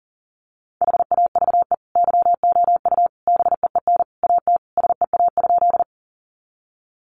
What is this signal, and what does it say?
Morse code, heard as a keyed tone, about 40 wpm, 700 Hz. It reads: HAVE YOU BEEN AT SEA?
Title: Sentences from Top 300 Words 40wpm